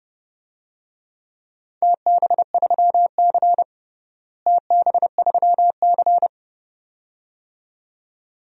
Morse code, heard as a keyed tone, about 32 wpm, 700 Hz. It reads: TB3C TB3C
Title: DXCC-100 Call Signs - 4 Characters - Repeated 2x - 30wpm